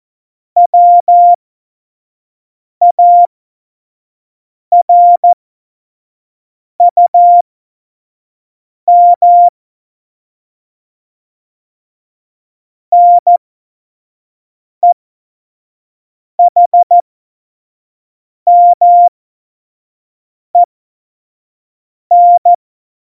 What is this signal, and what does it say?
Morse code, heard as a keyed tone, about 14 wpm, spaced as for 5 wpm, 700 Hz. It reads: WARUM NEHMEN
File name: Morsekurs / Witze / Witz 27. Oktober 2020, 25 F70